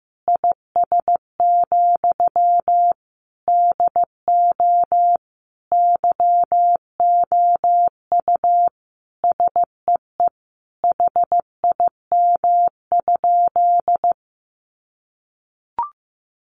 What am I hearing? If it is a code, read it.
Morse code, heard as a keyed tone, about 14 wpm, 700 Hz. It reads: IS, DO YOU SEE HIM?
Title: Sentences from Top 100 Words 15wpm